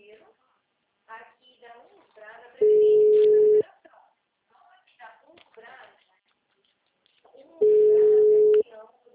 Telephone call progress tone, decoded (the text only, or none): ringback tone